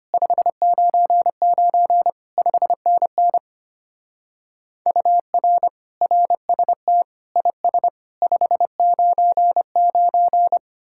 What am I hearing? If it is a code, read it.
Morse code, heard as a keyed tone, about 32 wpm, 700 Hz. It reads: 599 5NN UR RST IS 599